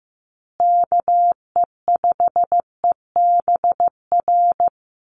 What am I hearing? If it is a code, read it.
Morse code, heard as a keyed tone, about 15 wpm, 700 Hz. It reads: KE5EBR
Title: USA Call Signs Repeated 2x 15wpm